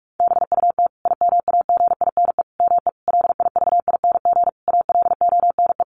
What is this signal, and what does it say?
Morse code, heard as a keyed tone, about 44 wpm, 700 Hz. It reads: BUT IMAGINE ME LIVING ALONE
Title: Sentences from Top 1400 Words 45wpm